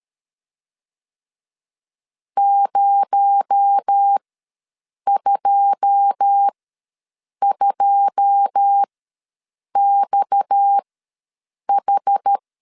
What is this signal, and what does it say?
Morse code, heard as a keyed tone, about 13 wpm, spaced as for 7 wpm, 800 Hz. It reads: 022XH
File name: LEZIONI / 9 2 ws 12wpm 3406